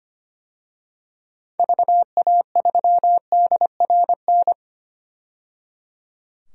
Morse code, heard as a keyed tone, about 25 wpm, 700 Hz. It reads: VA3DRN
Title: DXCC-100 Call Signs - 6 Characters - 25wpm